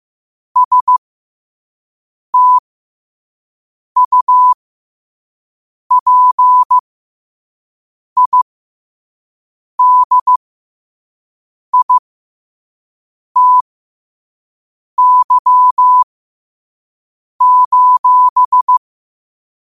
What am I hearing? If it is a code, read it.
Morse code, heard as a keyed tone, about 15 wpm, spaced as for 5 wpm, 1000 Hz. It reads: STUPIDITY:
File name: qotdfiles / 5wpm